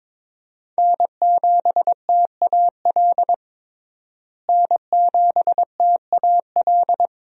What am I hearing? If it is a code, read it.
Morse code, heard as a keyed tone, about 21 wpm, 700 Hz. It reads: N7TAL N7TAL
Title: USA Call Signs - 5 Characters - Repeated 2x - 22wpm